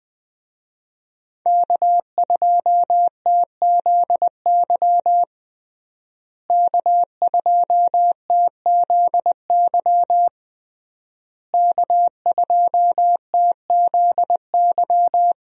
Morse code, heard as a keyed tone, about 20 wpm, 700 Hz. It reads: K2TZY K2TZY K2TZY
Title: USA Call Signs - 5 Characters - Repeated 3x - 20wpm